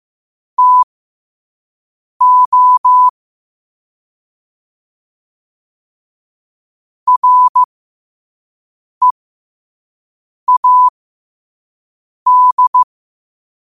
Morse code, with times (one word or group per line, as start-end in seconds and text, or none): 0.6-3.1 TO
7.1-12.8 READ